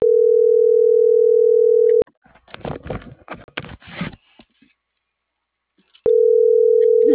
Ringback tone (North America), 440 + 480 Hz.